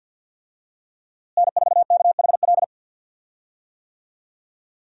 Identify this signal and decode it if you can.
Morse code, heard as a keyed tone, about 51 wpm, 700 Hz. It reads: N4KHL
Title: USA Call Signs - 5 Characters - Repeated 2x - 50wpm